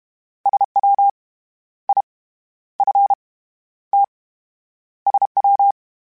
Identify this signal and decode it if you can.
Morse code, heard as a keyed tone, about 32 wpm, 800 Hz. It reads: SW I F T SW